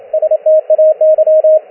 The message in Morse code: STAY